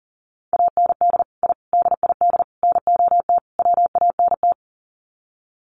Morse code, heard as a keyed tone, about 40 wpm, 700 Hz. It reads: AND I DID NOT WANT
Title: Sets of 5 Words 40wpm